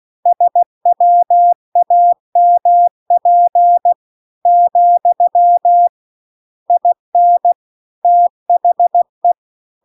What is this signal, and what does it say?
Morse code, heard as a keyed tone, about 16 wpm, 700 Hz. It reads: SWAMP , IN THE